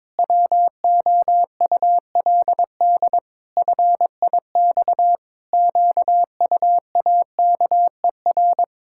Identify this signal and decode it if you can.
Morse code, heard as a keyed tone, about 22 wpm, 700 Hz. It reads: WOULD FIX QUAKER
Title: Pangram Challenge v2 - 22wpm